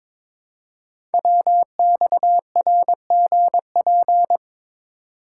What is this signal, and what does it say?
Morse code, heard as a keyed tone, about 22 wpm, 700 Hz. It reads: WXRGP
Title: Code Groups - 5 Letters - 22wpm